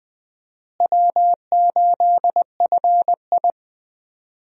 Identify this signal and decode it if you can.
Morse code, heard as a keyed tone, about 20 wpm, 700 Hz. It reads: W8FI
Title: USA Call Signs - 4 Characters - Repeated 3x - 20wpm